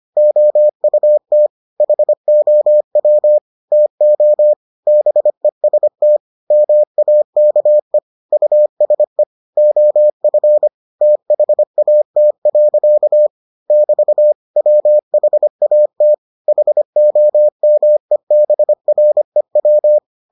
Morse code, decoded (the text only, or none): OUT HOW TO BEST MAKE USE OF THAT. <BT> WHAT HOMEBREW